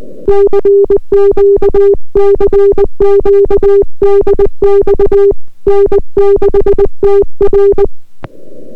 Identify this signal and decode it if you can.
Morse code, heard as a keyed tone, about 20 wpm, 400 Hz. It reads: CQ CQ DX N6TR